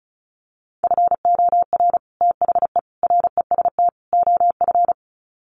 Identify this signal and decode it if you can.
Morse code, heard as a keyed tone, about 34 wpm, 700 Hz. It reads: FOR THE REST OF
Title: Sets of 4 Words 35wpm